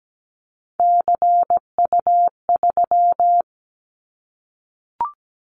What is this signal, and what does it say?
Morse code, heard as a keyed tone, about 17 wpm, 700 Hz. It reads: CU3
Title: Callsign Prefixes for Top 100 DXCC - 17wpm